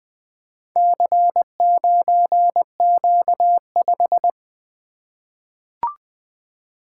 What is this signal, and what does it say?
Morse code, heard as a keyed tone, about 20 wpm, 700 Hz. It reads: C9Q5
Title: Four Letter-Number Combinations - 20wpm